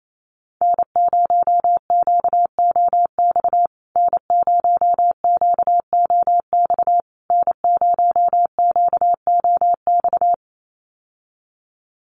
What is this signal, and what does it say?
Morse code, heard as a keyed tone, about 28 wpm, 700 Hz. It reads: N0QOX N0QOX N0QOX
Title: USA Call Signs Repeated 3x 28wpm